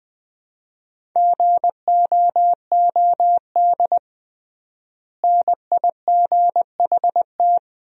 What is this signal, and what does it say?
Morse code, heard as a keyed tone, about 20 wpm, 700 Hz. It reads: GOOD NIGHT
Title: 3x Word-Spacing - Sentences from Top 300 Words - 20wpm